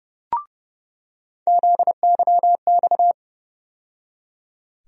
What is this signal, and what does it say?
Morse code, heard as a keyed tone, about 30 wpm, 700 Hz. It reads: ZYX
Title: USA Call Sign Postfixes - 3 Letters 30wpm